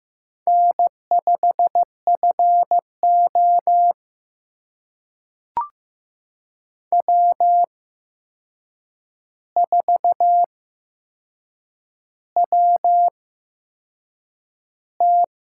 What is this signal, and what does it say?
Morse code, heard as a keyed tone, about 15 wpm, spaced as for 6 wpm, 700 Hz. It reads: N5FO W4WT
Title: USA Call Signs - 4 Characters - 8x Character Spacing - 15wpm